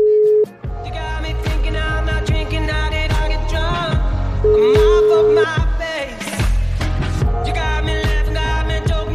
Ringback tone (Europe, 425 Hz), with noise.